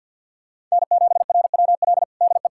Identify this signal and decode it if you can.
Morse code, heard as a keyed tone, about 46 wpm, 700 Hz. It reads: N7RWL DE
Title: USA Callsign DE Callsign 50wpm